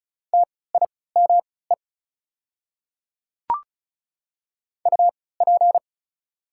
Morse code, heard as a keyed tone, about 36 wpm, spaced as for 20 wpm, 700 Hz. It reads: TIME UP
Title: 3x Character Spacing - Top 100 Words - 35wpm